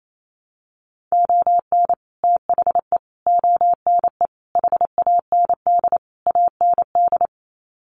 Morse code, heard as a keyed tone, about 28 wpm, 700 Hz. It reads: ON THE ONE HAND AND
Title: Sets of 5 Words Encore2 28wpm